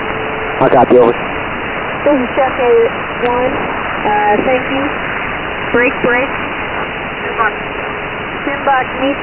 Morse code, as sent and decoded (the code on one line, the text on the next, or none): none